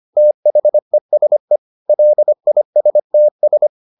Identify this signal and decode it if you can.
Morse code, heard as a keyed tone, about 25 wpm, 600 Hz. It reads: THESE LISTS